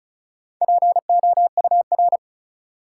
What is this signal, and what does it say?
Morse code, heard as a keyed tone, about 36 wpm, 700 Hz. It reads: POUR
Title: Four Letter Words - 35wpm